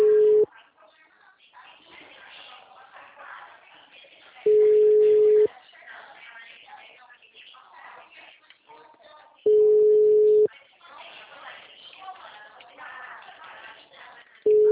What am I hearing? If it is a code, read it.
Ringback tone (Europe, 425 Hz), with speech.
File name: out- -2007- - - .8929